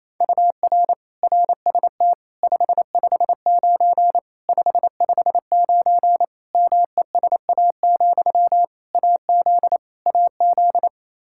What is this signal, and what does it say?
Morse code, heard as a keyed tone, about 28 wpm, 700 Hz. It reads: UR RST 559 559 MESA, AZ AZ
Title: Practice QSO - Exchange 1 - Beginning - 28wpm